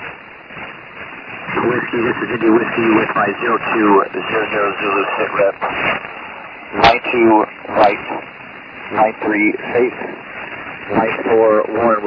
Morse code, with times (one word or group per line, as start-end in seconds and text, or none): none